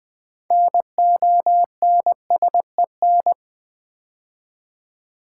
Morse code, heard as a keyed tone, about 20 wpm, 700 Hz. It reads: NONSEN
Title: Build A Word - 3 Letters At A Time - 8-Letter Words - 20wpm